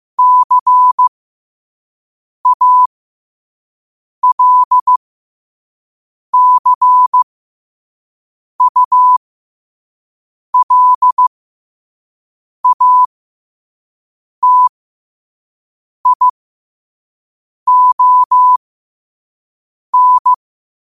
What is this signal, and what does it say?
Morse code, heard as a keyed tone, about 15 wpm, spaced as for 5 wpm, 1000 Hz. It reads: CALCULATION
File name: qotdfiles / 5wpm